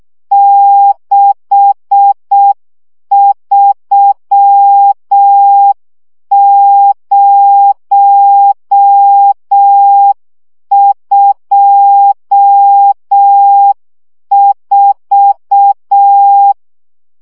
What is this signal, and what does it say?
Morse code, heard as a keyed tone, about 6 wpm, 800 Hz. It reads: 63024
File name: morze vizsga 2011-3 5